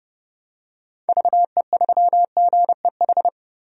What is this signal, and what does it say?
Morse code, heard as a keyed tone, about 32 wpm, 700 Hz. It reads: VE3GEH
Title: DXCC-100 Call Signs - 6 Characters - Repeated 3x - 30wpm